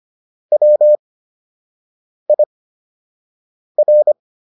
Morse code, heard as a keyed tone, about 25 wpm, spaced as for 6 wpm, 600 Hz. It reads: WIR